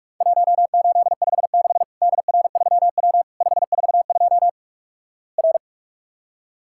Morse code, heard as a keyed tone, about 48 wpm, 700 Hz. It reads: 1856 DR3W 542 R
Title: Worked All Europe DX Contest - QTC Exchange - 45wpm